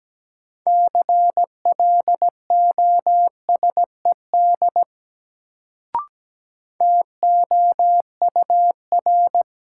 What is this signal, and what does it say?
Morse code, heard as a keyed tone, about 17 wpm, 700 Hz. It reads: CLOSED TOUR
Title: Sentences from Top 1400 Words - New Word Drill - 17wpm